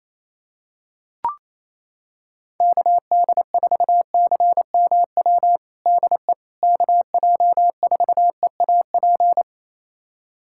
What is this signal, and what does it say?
Morse code, heard as a keyed tone, about 28 wpm, 700 Hz. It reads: KD4CMW DE KJ4EAP
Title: USA Callsign DE Callsign 28wpm